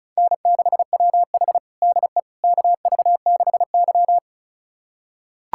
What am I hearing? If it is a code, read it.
Morse code, heard as a keyed tone, about 36 wpm, 700 Hz. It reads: N6WH DE KV6Y
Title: USA Callsign DE Callsign - 4 Characters - 35wpm